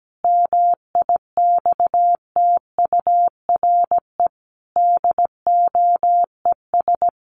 Morse code, heard as a keyed tone, about 17 wpm, 700 Hz. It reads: MIXTURE DOES
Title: Sentences from Top 1500 Words 17wpm